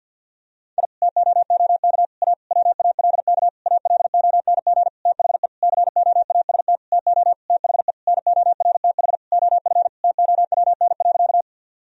Morse code, heard as a keyed tone, about 52 wpm, 700 Hz. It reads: I TOOK A WALK ALONG THE COAST TO THE NORTH OF TOWN.
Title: Sentences from Top 1000 Words - Review Part 4 - 50wpm